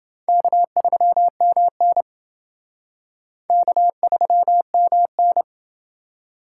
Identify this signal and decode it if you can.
Morse code, heard as a keyed tone, about 28 wpm, 700 Hz. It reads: K3MN K3MN